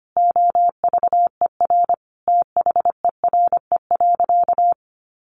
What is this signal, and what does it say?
Morse code, heard as a keyed tone, about 25 wpm, 700 Hz. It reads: OVER THERE.
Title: Sentences from Top 1000 Words 25wpm